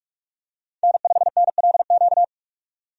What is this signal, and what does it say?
Morse code, heard as a keyed tone, about 45 wpm, 700 Hz. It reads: N5NLQ
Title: USA Call Signs - 5 Characters - 45wpm